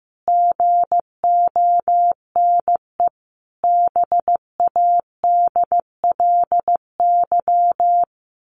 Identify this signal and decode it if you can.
Morse code, heard as a keyed tone, about 15 wpm, 700 Hz. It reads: GONE BADLY
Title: A Sound Of Thunder 15wpm